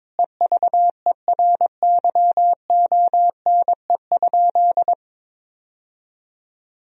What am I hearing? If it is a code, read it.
Morse code, heard as a keyed tone, about 21 wpm, 700 Hz. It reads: EVERYONE?
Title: Sentences from Top 1000 Words - Review Part 4 - 22wpm